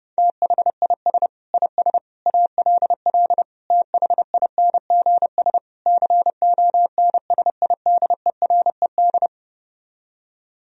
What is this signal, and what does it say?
Morse code, heard as a keyed tone, about 33 wpm, 700 Hz. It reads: THIS IS ALL THINGS CONSIDERED